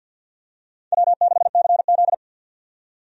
Morse code, heard as a keyed tone, about 50 wpm, 700 Hz. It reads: W6CZ
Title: USA Call Signs - 4 Characters - 50wpm